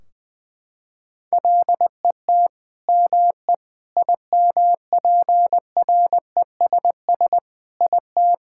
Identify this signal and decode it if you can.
Morse code, heard as a keyed tone, about 20 wpm, 700 Hz. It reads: LET ME IMPRESS IT
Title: Sentences from Top 1600 Words 20wpm